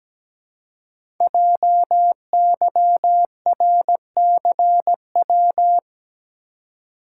Morse code, heard as a keyed tone, about 17 wpm, 700 Hz. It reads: JYRCW
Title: Code Groups - 5 Letter:Numbers - 17wpm